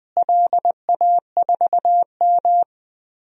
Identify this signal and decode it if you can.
Morse code, heard as a keyed tone, about 21 wpm, 700 Hz. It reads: LA4M
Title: DXCC-100 Call Signs - 4 Characters - Repeated 3x - 20wpm